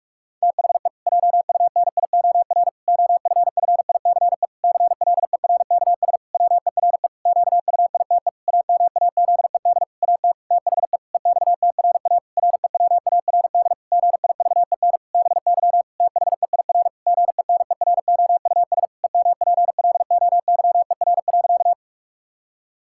Morse code, heard as a keyed tone, about 46 wpm, 700 Hz. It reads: THE JUNIOR OFFICE CLERKS WERE QUITE AMAZED AT THE EXTRA REWARD GIVEN BY THEIR GENEROUS EMPLOYER.
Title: Pangram Challenge v2 - 45wpm